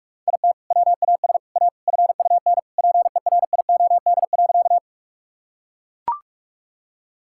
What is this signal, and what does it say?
Morse code, heard as a keyed tone, about 41 wpm, 690 Hz. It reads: IT WAS A FUN PERIOD.
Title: Sentences from Top 1000 Words - Review Part 5 - 45wpm